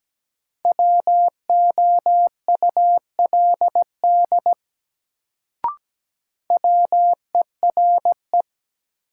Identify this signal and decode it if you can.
Morse code, heard as a keyed tone, about 17 wpm, 700 Hz. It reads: WOULD WERE
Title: Sets of 2 Words - New Word Drill - 17wpm